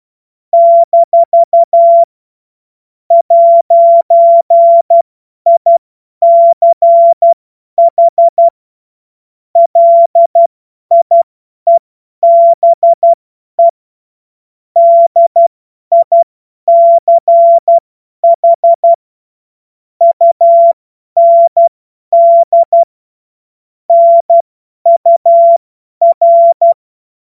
Morse code, heard as a keyed tone, about 12 wpm, spaced as for 10 wpm, 680 Hz. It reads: - 'ICH LIEBE DICH UND NUR
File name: Morsekurs / Witze / Witz 07. Juni 2022 50 F70